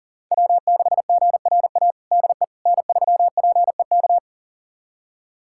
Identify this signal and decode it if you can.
Morse code, heard as a keyed tone, about 40 wpm, 700 Hz. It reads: W6GRA DE N3PEK
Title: USA Callsign DE Callsign 40wpm